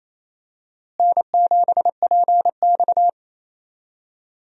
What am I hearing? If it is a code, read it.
Morse code, heard as a keyed tone, about 28 wpm, 700 Hz. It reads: N7PX
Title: USA Call Signs - 4 Characters - Repeated 2x - 28wpm